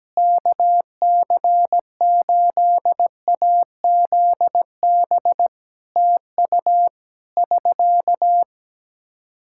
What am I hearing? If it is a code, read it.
Morse code, heard as a keyed tone, about 17 wpm, 700 Hz. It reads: KC8AZB TU <SK>
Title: Practice QSO - Exchange 4 - Ending - 17wpm